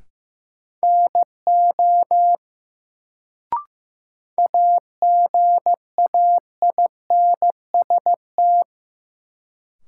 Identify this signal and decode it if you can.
Morse code, heard as a keyed tone, about 15 wpm, 700 Hz. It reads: NO AGAINST
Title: Sets of 2 Words - New Word Drill - 15wpm